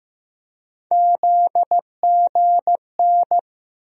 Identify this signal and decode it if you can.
Morse code, heard as a keyed tone, about 15 wpm, 700 Hz. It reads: ZGN
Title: USA Call Sign Postfixes - 3 Letters 15wpm